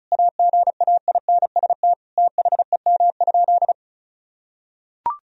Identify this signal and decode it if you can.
Morse code, heard as a keyed tone, about 36 wpm, 700 Hz. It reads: AGAINST THEM?
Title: Sentences from Top 1000 Words - Review Part 2 - 35wpm